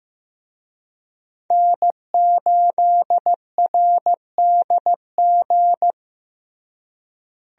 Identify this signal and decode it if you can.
Morse code, heard as a keyed tone, about 15 wpm, 700 Hz. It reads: N8RDG
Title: USA Call Signs - 5 Characters - Repeated 2x - 15wpm